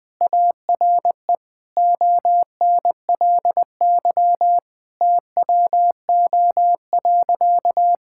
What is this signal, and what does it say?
Morse code, heard as a keyed tone, about 21 wpm, 700 Hz. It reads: ARE ONLY TWO.
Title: Sentences from Top 1000 Words - Review Part 5 - 20wpm